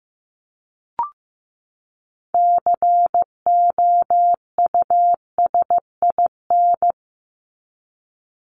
Morse code, heard as a keyed tone, about 15 wpm, 700 Hz. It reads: COUSIN
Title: Top 1501-1750 Words 15wpm